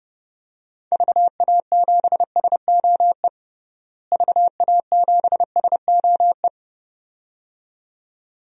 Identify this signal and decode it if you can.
Morse code, heard as a keyed tone, about 30 wpm, 700 Hz. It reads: VA7SOE VA7SOE
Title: DXCC-100 Call Signs - 6 Characters - Repeated 2x - 30wpm